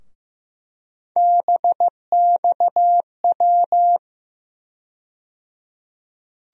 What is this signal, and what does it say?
Morse code, heard as a keyed tone, about 15 wpm, 700 Hz. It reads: BXW
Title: USA Call Sign Postfixes - 3 Letters 15wpm